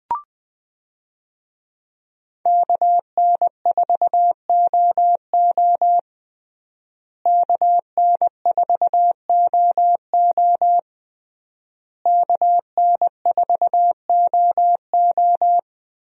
Morse code, heard as a keyed tone, about 20 wpm, 700 Hz. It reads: KN4OO KN4OO KN4OO
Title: USA Call Signs - 5 Characters - Repeated 3x - 20wpm